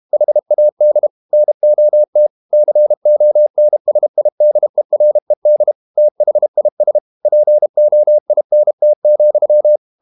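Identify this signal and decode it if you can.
Morse code, heard as a keyed tone, about 32 wpm, 600 Hz. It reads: HAD NOT CONSIDERED THIS POINT,
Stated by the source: Morse code practice at 32 wpm from CW med Gnister, reading Ken Follett's novel Never in English.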